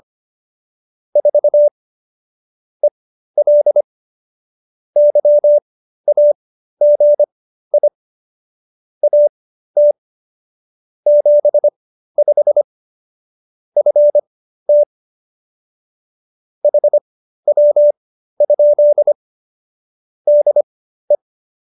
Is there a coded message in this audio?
Yes: Morse code from 1.2 to 21.2 s, heard as a keyed tone, about 25 wpm, spaced as for 13 wpm, 590 Hz. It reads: 4 EL YAGI AT 75 FT HW? DE